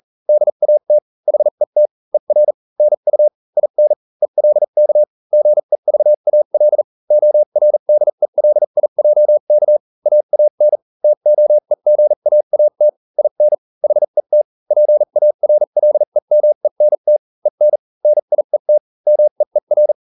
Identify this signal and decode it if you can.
Morse code, heard as a keyed tone, about 40 wpm, spaced as for 35 wpm, 600 Hz. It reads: DAT HET ER NU IN ELK GEVAL ORDELIJK AAN TOEGAAT IN HET PARLEMENT EN NIET MEER